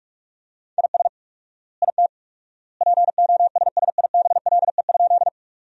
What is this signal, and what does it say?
Morse code, heard as a keyed tone, about 46 wpm, 690 Hz. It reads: IS IT POSSIBLE?
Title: 4x Word-Spacing - Sentences from Top 500 Words - 45wpm